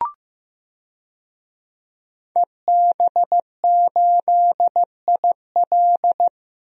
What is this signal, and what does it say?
Morse code, heard as a keyed tone, about 15 wpm, 700 Hz. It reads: EB8IL
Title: DXCC-100 Call Signs - 5 Characters - 15wpm